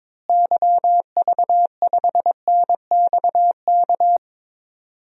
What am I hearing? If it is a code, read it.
Morse code, heard as a keyed tone, about 22 wpm, 700 Hz. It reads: YV5NXK
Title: DXCC-100 Call Signs - 6 Characters - Repeated 2x - 22wpm